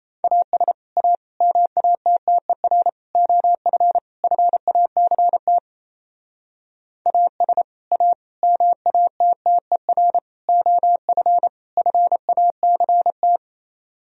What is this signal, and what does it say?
Morse code, heard as a keyed tone, about 37 wpm, 700 Hz. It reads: AS A MATTER OF FACT AS A MATTER OF FACT